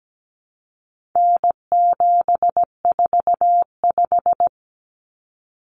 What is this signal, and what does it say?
Morse code, heard as a keyed tone, about 17 wpm, 700 Hz. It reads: N745
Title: Common Cut Numbers - 4-Digit 17wpm